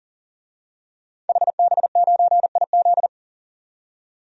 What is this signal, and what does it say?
Morse code, heard as a keyed tone, about 40 wpm, 700 Hz. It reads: HB9IZ
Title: DXCC-100 Call Signs - 5 Characters - 40wpm